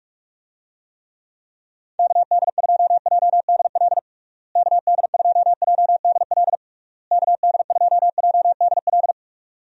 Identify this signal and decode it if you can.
Morse code, heard as a keyed tone, about 46 wpm, 700 Hz. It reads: KD2JDL KD2JDL KD2JDL
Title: USA Call Signs - 6 Characters - Repeated 3x - 45wpm